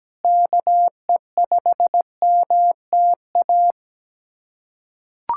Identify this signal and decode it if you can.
Morse code, heard as a keyed tone, about 17 wpm, 700 Hz. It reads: KE5MTA
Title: USA Callsign DE Callsign 17wpm